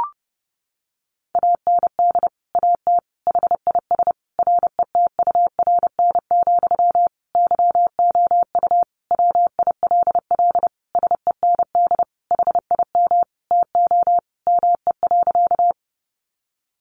Morse code, heard as a keyed tone, about 30 wpm, 700 Hz. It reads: AND AT HIS RETURN, YOU WILL SEND HIM TO ME.
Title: Sentences from Top 1500 Words 30wpm